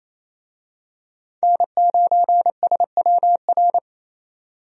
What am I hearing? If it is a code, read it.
Morse code, heard as a keyed tone, about 28 wpm, 700 Hz. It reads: N9SWR
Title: USA Call Signs - 5 Characters - 28wpm